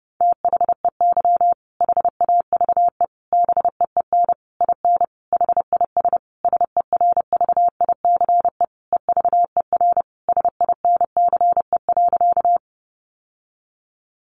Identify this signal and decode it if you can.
Morse code, heard as a keyed tone, about 30 wpm, 700 Hz. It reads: THEY HAVE BEEN IN HIS SERVICE EVER SINCE.
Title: Sentences from Top 700 Words 30wpm